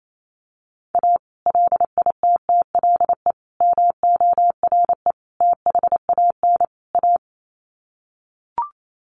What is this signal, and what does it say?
Morse code, heard as a keyed tone, about 28 wpm, 700 Hz. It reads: A LITTLE MORE THAN A
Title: Sets of 5 Words Encore2 28wpm